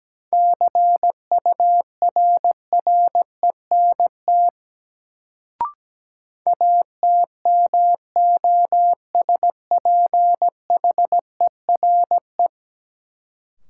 Morse code, heard as a keyed tone, about 17 wpm, 700 Hz. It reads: CURRENT ATMOSPHERE
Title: Sentences from Top 1600 Words - New Word Drill Alt - 17wpm